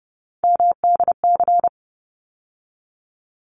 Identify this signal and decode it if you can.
Morse code, heard as a keyed tone, about 30 wpm, 700 Hz. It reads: MDC
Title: ARRL Sections 30wpm